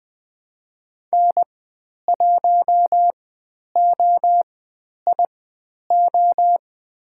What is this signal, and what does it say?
Morse code, heard as a keyed tone, about 20 wpm, spaced as for 10 wpm, 700 Hz. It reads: N1OIO
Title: 095 Lesson 16g - Letter c - Call Signs 20wpm